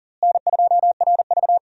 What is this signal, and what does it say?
Morse code, heard as a keyed tone, about 40 wpm, 700 Hz. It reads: N2RV
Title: USA Call Signs - 4 Characters - Repeated 3x - 40wpm